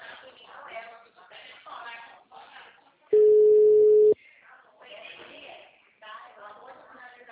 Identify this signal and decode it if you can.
Ringback tone (Europe, 425 Hz), with speech.
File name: out- -2010- - - .19603